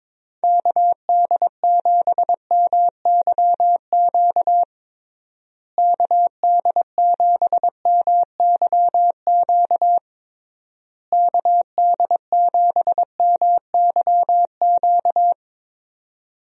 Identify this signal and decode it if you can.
Morse code, heard as a keyed tone, about 22 wpm, 700 Hz. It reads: KD7MYQ KD7MYQ KD7MYQ
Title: USA Call Signs - 6 Characters - Repeated 3x - 22wpm